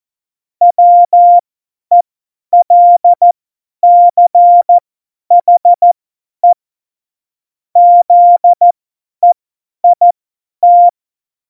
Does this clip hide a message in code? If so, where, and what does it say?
Yes: Morse code from 0.6 to 10.9 s, heard as a keyed tone, about 14 wpm, spaced as for 10 wpm, 700 Hz. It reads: WELCHE ZEIT